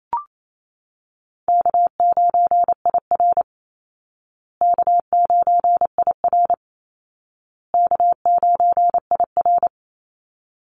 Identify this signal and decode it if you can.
Morse code, heard as a keyed tone, about 28 wpm, 700 Hz. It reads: K9IR K9IR K9IR
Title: Call Signs Repeated 2x - 28wpm